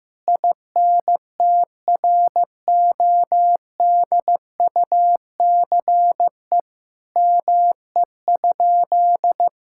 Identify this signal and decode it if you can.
Morse code, heard as a keyed tone, about 15 wpm, 700 Hz. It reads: INTRODUCE ME?
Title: Sentences from Top 1600 Words 15wpm